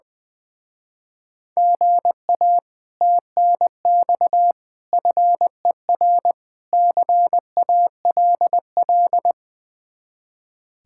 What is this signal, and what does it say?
Morse code, heard as a keyed tone, about 20 wpm, 700 Hz. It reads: GA TNX FER CALL
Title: Speed-Racing Common QSO Phrases - 20wpm